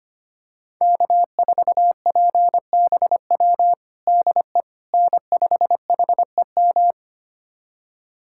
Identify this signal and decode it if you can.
Morse code, heard as a keyed tone, about 25 wpm, 700 Hz. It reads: K4PBW DE N5HEM
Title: USA Callsign DE Callsign - 5 Characters - 25wpm